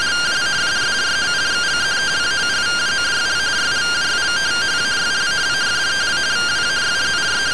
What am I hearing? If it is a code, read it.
Two-tone teleprinter signal, about 170 Hz shift.
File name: findings / MR AXIAL 71mm 3mm.gen-06 rtty rx